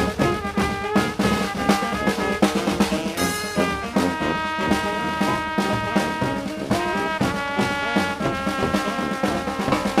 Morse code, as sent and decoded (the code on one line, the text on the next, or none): none